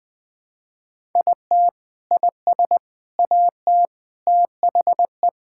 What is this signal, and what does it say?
Morse code, heard as a keyed tone, about 20 wpm, 700 Hz. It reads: IT IS AT THE
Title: Sentences with 4-Word Phrases - Spoken Beginning - 20wpm